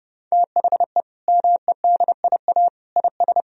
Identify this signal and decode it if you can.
Morse code, heard as a keyed tone, about 32 wpm, 700 Hz. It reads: THE MEDIA IS
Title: Sentences from Top 1000 Words - Review Part 4 - 30wpm